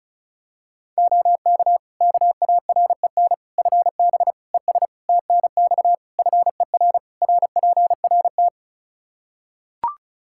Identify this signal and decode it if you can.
Morse code, heard as a keyed tone, about 36 wpm, 700 Hz. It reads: OK KAREN FB ES TNX FER RPRT
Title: Practice QSO - Exchange 2 - Info - 35wpm